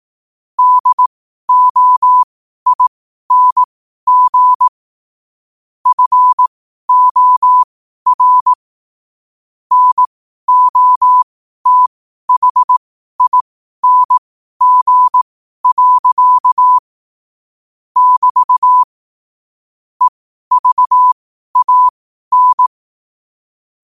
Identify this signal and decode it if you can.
Morse code, heard as a keyed tone, about 18 wpm, spaced as for 12 wpm, 1000 Hz. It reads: DOING FOR NOTHING. <BT> EVAN
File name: qotdfiles / 12wpm